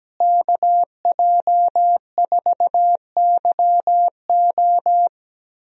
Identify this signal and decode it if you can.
Morse code, heard as a keyed tone, about 17 wpm, 700 Hz. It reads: KJ4YO
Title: USA Call Signs - 5 Characters - Repeated 2x - 17wpm